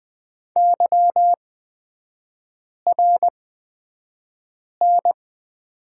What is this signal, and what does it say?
Morse code, heard as a keyed tone, about 20 wpm, spaced as for 5 wpm, 700 Hz. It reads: YRN
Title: 224 Lesson 36e - BT - Three Characters 20wpm